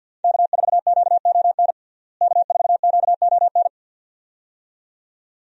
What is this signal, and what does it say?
Morse code, heard as a keyed tone, about 48 wpm, 700 Hz. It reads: K4QON K4QON
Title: USA Call Signs - 5 Characters - Repeated 2x - 50wpm